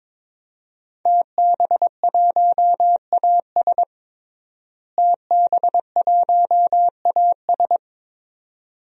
Morse code, heard as a keyed tone, about 22 wpm, 700 Hz. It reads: TB1AS TB1AS
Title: DXCC-100 Call Signs - 5 Characters - Repeated 2x - 22wpm